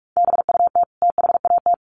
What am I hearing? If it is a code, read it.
Morse code, heard as a keyed tone, about 44 wpm, 700 Hz. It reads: BUT THAT
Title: Sets of 2 Words 45wpm